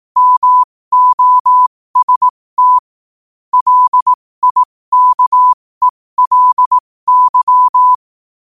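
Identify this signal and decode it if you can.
Morse code, heard as a keyed tone, about 18 wpm, spaced as for 15 wpm, 1000 Hz. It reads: MOST LIKELY
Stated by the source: Morse practice sent at 15 wpm.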